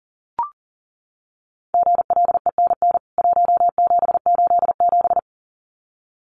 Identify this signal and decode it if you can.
Morse code, heard as a keyed tone, about 40 wpm, 700 Hz. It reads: GLENN 1787
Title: CWT Exchange 40wpm